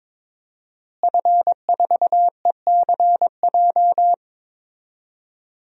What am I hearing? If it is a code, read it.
Morse code, heard as a keyed tone, about 22 wpm, 700 Hz. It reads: F4ECJ
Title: DXCC-100 Call Signs - 5 Characters - Repeated 2x - 22wpm